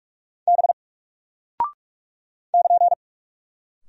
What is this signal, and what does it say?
Morse code, heard as a keyed tone, about 44 wpm, 700 Hz. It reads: B <KN>
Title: Single Character (Letter:Number:Punctuation:Prosign) Alt 45wpm